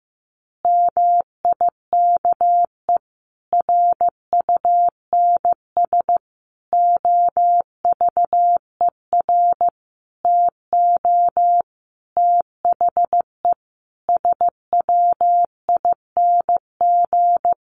Morse code, heard as a keyed tone, about 14 wpm, 700 Hz. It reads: MIKE RUNS OVER TO THE SWING
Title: Additional Easy Phrases 15wpm